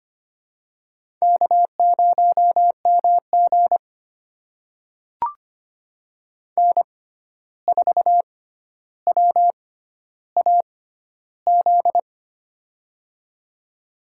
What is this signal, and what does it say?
Morse code, heard as a keyed tone, about 25 wpm, spaced as for 12 wpm, 700 Hz. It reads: K0MG N4WAZ
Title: USA Call Signs - 6x Character Spacing - 25wpm